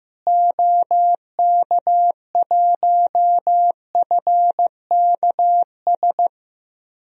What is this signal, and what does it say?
Morse code, heard as a keyed tone, about 15 wpm, 700 Hz. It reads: OK1FKS
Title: DXCC-100 Call Signs - 6 Characters - Repeated 2x - 15wpm